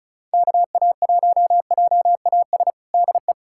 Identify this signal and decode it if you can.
Morse code, heard as a keyed tone, about 36 wpm, 700 Hz. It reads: KA1JAS DE
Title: USA Callsign DE Callsign - 6 Characters - 35wpm